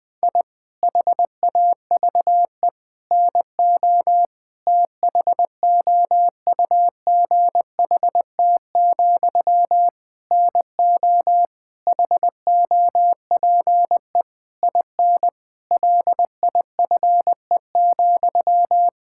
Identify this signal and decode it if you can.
Morse code, heard as a keyed tone, about 21 wpm, 700 Hz. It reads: I HAVE NO THOUGHT, NO HOPE IN LIFE,
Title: Sentences from Top 1000 Words - Review Part 6 - 20wpm